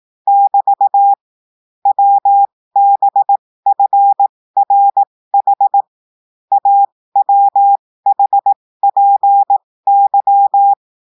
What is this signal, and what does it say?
Morse code, heard as a keyed tone, about 18 wpm, spaced as for 15 wpm, 800 Hz. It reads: <BT> WBFRH AWHPY
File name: cours-du-lundi files / cours-1580-texte-chiffre-15-mpm